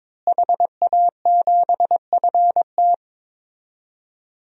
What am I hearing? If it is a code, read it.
Morse code, heard as a keyed tone, about 21 wpm, 700 Hz. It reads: HA7FT
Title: DXCC-100 Call Signs - 5 Characters - Repeated 3x - 22wpm